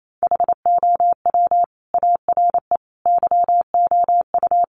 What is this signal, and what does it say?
Morse code, heard as a keyed tone, about 28 wpm, 700 Hz. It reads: HOW ARE YOU
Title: Sentences from Top 1200 Words 28wpm